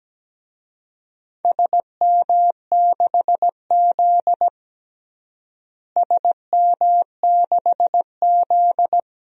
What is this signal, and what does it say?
Morse code, heard as a keyed tone, about 17 wpm, 700 Hz. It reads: SM6Z SM6Z
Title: DXCC-100 Call Signs - 4 Characters - Repeated 2x - 17wpm